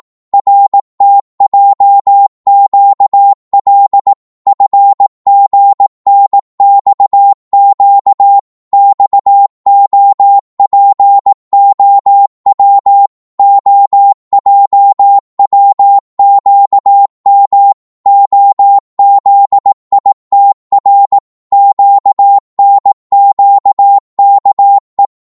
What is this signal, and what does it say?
Morse code, heard as a keyed tone, about 18 wpm, 800 Hz. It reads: RTJQL FGNXQ XOPOW OJWQM OZITR QNQKE
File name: Morse / ltr-18wpm